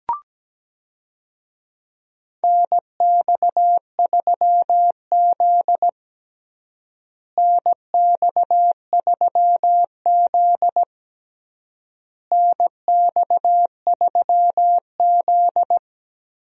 Morse code, heard as a keyed tone, about 17 wpm, 700 Hz. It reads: NX3Z NX3Z NX3Z
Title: USA Call Signs - 4 Characters - Repeated 3x - 17wpm